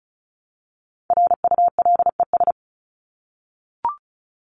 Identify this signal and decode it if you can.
Morse code, heard as a keyed tone, about 34 wpm, 700 Hz. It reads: RULES
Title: ES:S Suffix - Top 100 Words with Base Words 35wpm